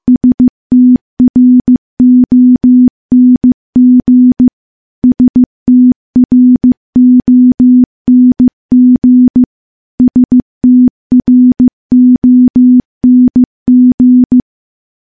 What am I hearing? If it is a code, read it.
Morse code, heard as a keyed tone, about 15 wpm, 250 Hz. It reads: STRONG STRONG STRONG
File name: morse / morse-training-15WPM